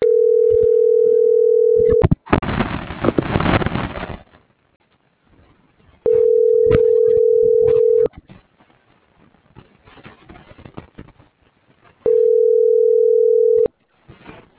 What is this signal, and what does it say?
Ringback tone (North America), 440 + 480 Hz.